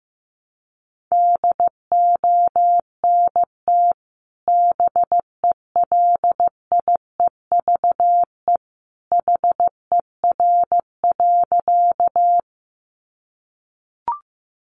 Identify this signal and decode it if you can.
Morse code, heard as a keyed tone, about 15 wpm, 700 Hz. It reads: DONT BELIEVE HER.
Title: Sentences from Top 1000 Words 15wpm